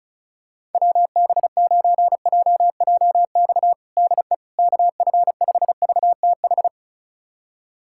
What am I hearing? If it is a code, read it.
Morse code, heard as a keyed tone, about 36 wpm, 700 Hz. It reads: WB9JJX DE KF5VTH
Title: USA Callsign DE Callsign - 6 Characters - 35wpm